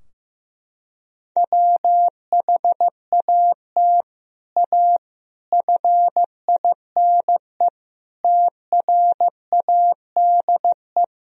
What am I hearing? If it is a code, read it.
Morse code, heard as a keyed tone, about 16 wpm, 700 Hz. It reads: WHAT A FINE TRADE
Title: Sentences from Top 1000 Words - Review Part 4 - 15wpm